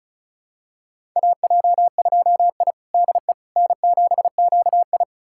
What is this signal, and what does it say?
Morse code, heard as a keyed tone, about 36 wpm, 700 Hz. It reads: AJ2I DE N7QI
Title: USA Callsign DE Callsign - 4 Characters - 35wpm